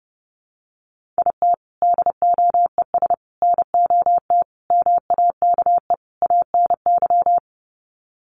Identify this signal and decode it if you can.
Morse code, heard as a keyed tone, about 30 wpm, 700 Hz. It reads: IT DOES NOT MAKE ANY
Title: Sets of 5 Words Encore2 30wpm